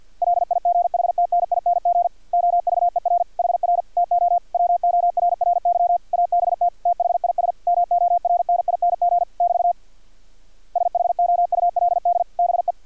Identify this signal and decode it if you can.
Morse code, heard as a keyed tone, about 50 wpm, 700 Hz. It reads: LIGHTNING OVER HR TO WORRY ABT THIS MORNING <BT> SHOULD BE
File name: archived / CRSnet